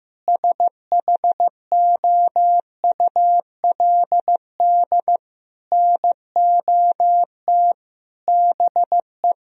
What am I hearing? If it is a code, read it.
Morse code, heard as a keyed tone, about 16 wpm, 700 Hz. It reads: SHOULD NOT BE
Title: Sentences from Top 1000 Words - Review Part 3 - 15wpm